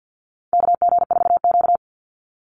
Morse code, heard as a keyed tone, about 50 wpm, 700 Hz. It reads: KG4Q
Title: Call Signs - 50wpm